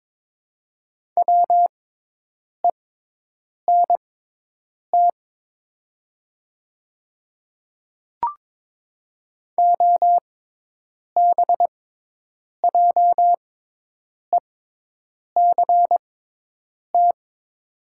Morse code, heard as a keyed tone, about 22 wpm, spaced as for 8 wpm, 700 Hz. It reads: WENT OBJECT
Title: 6x Character Spacing - Top 500 Words - 22wpm